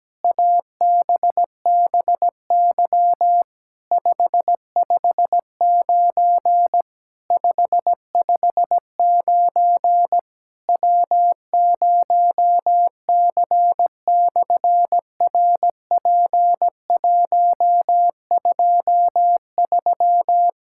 Morse code, heard as a keyed tone, about 17 wpm, 700 Hz. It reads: ABBY 559 559 W0C/RP123
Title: SOTA QSO Practice 17wpm